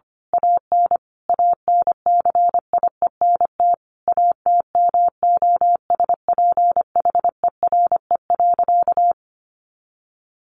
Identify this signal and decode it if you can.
Morse code, heard as a keyed tone, about 25 wpm, 700 Hz. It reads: AN ANCIENT ATMOSPHERE.
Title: A Sound Of Thunder 25wpm